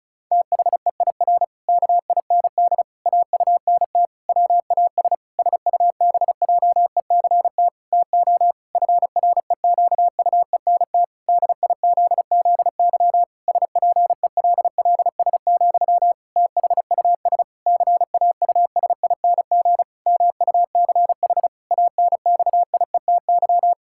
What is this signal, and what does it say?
Morse code, heard as a keyed tone, about 34 wpm, 700 Hz. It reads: THEIR KIND AUNT WAS SUBJECT TO FREQUENT DIZZY SPELLS, THUS CAUSING MUCH ANXIETY
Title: Pangram Challenge v2 - 35wpm